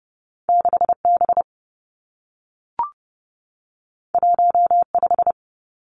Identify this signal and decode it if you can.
Morse code, heard as a keyed tone, about 30 wpm, 700 Hz. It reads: 6 B 1 5
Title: Two Challenging Letter-Number Combinations 30wpm